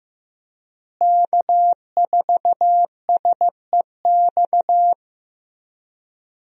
Morse code, heard as a keyed tone, about 15 wpm, 700 Hz. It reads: K4SEX
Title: USA Call Signs - 5 Characters - Repeated 3x - 15wpm